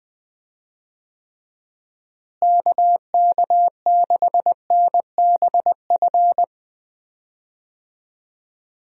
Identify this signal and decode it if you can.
Morse code, heard as a keyed tone, about 20 wpm, 700 Hz. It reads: KK6NBF
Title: USA Call Signs - 6 Characters - Repeated 2x - 20wpm